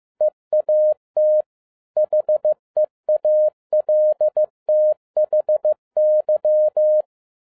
Morse code, heard as a keyed tone, about 15 wpm, 610 Hz. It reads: EAT HEALTHY
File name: 2011-07-mp3 / n57 BACK FROM A BIG ACCIDENT 15wpm